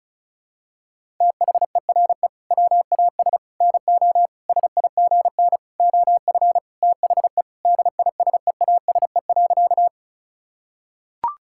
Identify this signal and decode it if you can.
Morse code, heard as a keyed tone, about 34 wpm, 700 Hz. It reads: THERE WAS NO SIGN OF THE DISEASE.
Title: Sentences from Top 1000 Words - Review Part 6 - 35wpm